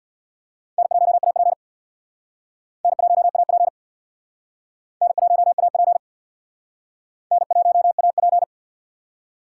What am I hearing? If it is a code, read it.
Morse code, heard as a keyed tone, about 62 wpm, 700 Hz. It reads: N1AP N1AP N1AP N1AP
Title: Speed Racing - USA 1x2 Call Signs 50wpm